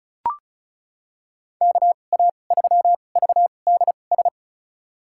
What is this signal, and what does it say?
Morse code, heard as a keyed tone, about 36 wpm, spaced as for 25 wpm, 700 Hz. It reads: KA3VDS
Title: USA Call Signs - 6 Characters - 2x Character Spacing - 35wpm